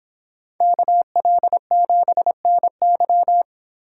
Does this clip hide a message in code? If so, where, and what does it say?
Yes: Morse code from 0.6 to 3.4 s, heard as a keyed tone, about 30 wpm, 700 Hz. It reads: KL7NY